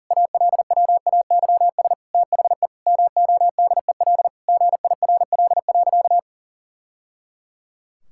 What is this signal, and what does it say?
Morse code, heard as a keyed tone, about 40 wpm, 700 Hz. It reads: ALWAYS THE MODEL GIRL.
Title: Sentences from Top 1000 Words - Review Part 5 - 40wpm